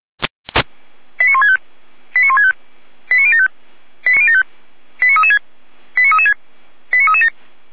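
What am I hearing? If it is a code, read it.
Five-tone selective call (ZVEI), heard as quick runs of five tones, tones of about 70 ms. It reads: 89165 89165 89075 89336 89336 89337
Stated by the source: ZVEI five-tone selective call received on an SDR.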